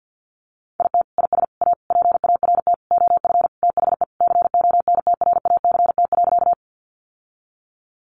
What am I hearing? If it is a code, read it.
Morse code, heard as a keyed tone, about 48 wpm, 700 Hz. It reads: IT IS A PART OF THE CONTRACT.
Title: Sentences from Top 1400 Words 50wpm